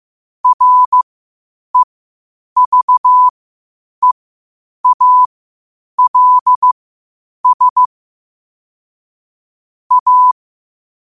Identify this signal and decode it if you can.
Morse code, heard as a keyed tone, about 15 wpm, spaced as for 8 wpm, 1000 Hz. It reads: REVEALS A
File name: qotdfiles / 8wpm